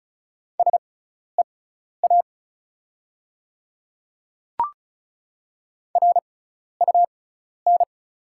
Morse code, heard as a keyed tone, about 34 wpm, spaced as for 12 wpm, 690 Hz. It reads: SEA RUN
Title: 6x Character Spacing - Top 300 Words - 35wpm